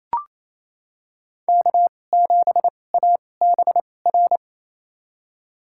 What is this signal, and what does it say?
Morse code, heard as a keyed tone, about 28 wpm, spaced as for 20 wpm, 700 Hz. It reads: K7ABR
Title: USA Call Signs - 5 Characters - 2x Character Spacing - 28wpm